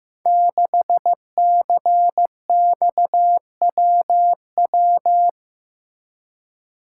Morse code, heard as a keyed tone, about 15 wpm, 700 Hz. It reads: 6CXWW
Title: Code Groups - 5 Letter:Numbers - 15wpm